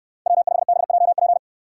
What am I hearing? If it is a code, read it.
Morse code, heard as a keyed tone, about 67 wpm, 700 Hz. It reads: W5BJP